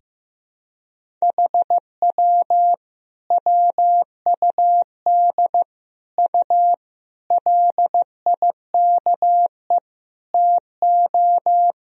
Morse code, heard as a keyed tone, about 15 wpm, 700 Hz. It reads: HW WUD U LIKE TO
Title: Sentences from Top 100 Words and Abbreviations 15wpm